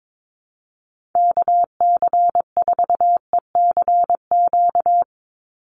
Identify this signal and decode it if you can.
Morse code, heard as a keyed tone, about 22 wpm, 700 Hz. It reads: KC4ECQ
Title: USA Call Signs 22wpm